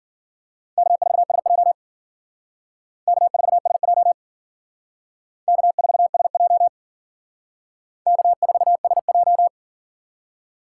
Morse code, heard as a keyed tone, about 51 wpm, 700 Hz. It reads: K4SJ K4SJ K4SJ K4SJ